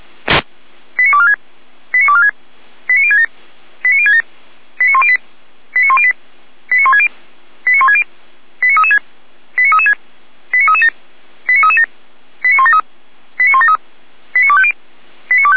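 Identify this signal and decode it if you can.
Five-tone selective call (ZVEI), heard as quick runs of five tones, tones of about 70 ms. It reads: 89246 89246 89067 89067 89108 89108 89160 89160 89336 89336 89337 89337 89173 89173 89250
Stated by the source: ZVEI five-tone selective call received on SDR.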